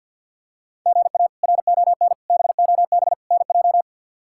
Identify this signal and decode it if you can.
Morse code, heard as a keyed tone, about 49 wpm, 700 Hz. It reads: GA RON BOB NJ
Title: K1USN SST QSO Practice - 50wpm